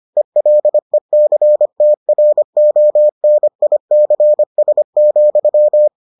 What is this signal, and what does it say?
Morse code, heard as a keyed tone, about 25 wpm, 600 Hz. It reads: ELECTRONICS,